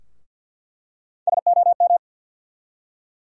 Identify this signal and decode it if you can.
Morse code, heard as a keyed tone, about 53 wpm, 700 Hz. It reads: SOM
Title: USA Call Sign Postfixes - 3 Letters 50wpm